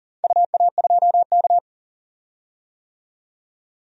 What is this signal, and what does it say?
Morse code, heard as a keyed tone, about 40 wpm, 700 Hz. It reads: UA2K
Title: DXCC-100 Call Signs - 4 Characters - 40wpm